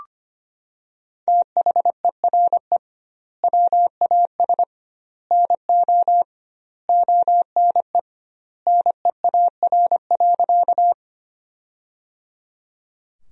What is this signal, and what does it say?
Morse code, heard as a keyed tone, about 25 wpm, 700 Hz. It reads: THERE WAS NO ONE NEAR.
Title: 2x Word-Spacing - Sentences from Top 200 Words - 25wpm